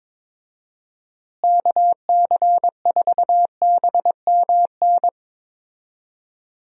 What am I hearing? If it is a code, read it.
Morse code, heard as a keyed tone, about 22 wpm, 700 Hz. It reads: KC4BMN
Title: USA Call Signs - 6 Characters - Repeated 3x - 22wpm